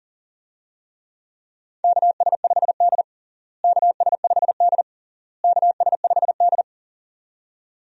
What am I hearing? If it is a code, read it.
Morse code, heard as a keyed tone, about 40 wpm, 700 Hz. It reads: KS5D KS5D KS5D
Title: USA Call Signs - 4 Characters - Repeated 3x - 40wpm